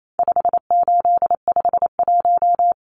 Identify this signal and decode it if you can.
Morse code, heard as a keyed tone, about 28 wpm, 700 Hz. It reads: 5851
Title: Numbers - 4-Digit 28wpm